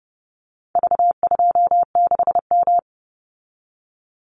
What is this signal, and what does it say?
Morse code, heard as a keyed tone, about 28 wpm, 700 Hz. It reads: V26M
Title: Call Signs Repeated 2x - 30wpm